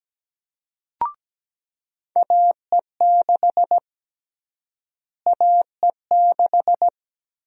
Morse code, heard as a keyed tone, about 17 wpm, 700 Hz. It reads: AE6 AE6
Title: USA Call Sign Prefixes - Repeated 2x - 17wpm